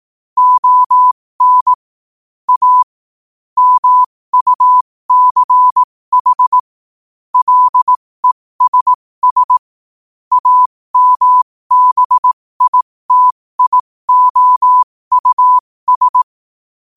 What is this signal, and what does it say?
Morse code, heard as a keyed tone, about 18 wpm, spaced as for 15 wpm, 1000 Hz. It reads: ON A MUCH LESS AMBITIOUS
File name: qotdfiles / 15wpm